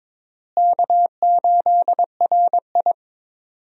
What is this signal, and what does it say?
Morse code, heard as a keyed tone, about 21 wpm, 700 Hz. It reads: K8RI
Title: USA Call Signs - 4 Characters - Repeated 3x - 22wpm